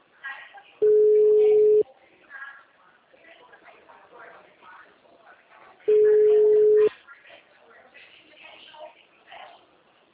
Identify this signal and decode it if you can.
Ringback tone (Europe, 425 Hz), with speech.